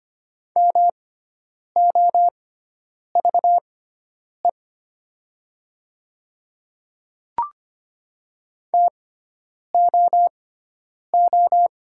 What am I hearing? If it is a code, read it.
Morse code, heard as a keyed tone, about 25 wpm, spaced as for 9 wpm, 700 Hz. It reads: MOVE TOO
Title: 6x Character Spacing - Top 200 Words - 25wpm